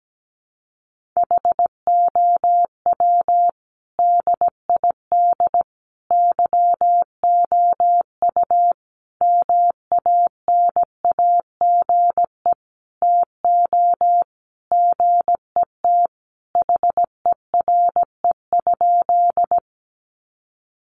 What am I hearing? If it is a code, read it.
Morse code, heard as a keyed tone, about 17 wpm, 700 Hz. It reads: HOW DID YOU MANAGE TO GET HERE?
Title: Sentences from Top 1400 Words 17wpm